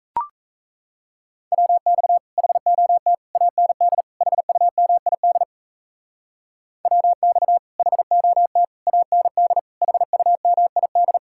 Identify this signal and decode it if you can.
Morse code, heard as a keyed tone, about 34 wpm, 700 Hz. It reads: WX HOT AND HUMID WX HOT AND HUMID
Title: Speed-Racing Common QSO Phrases Encore - 28wpm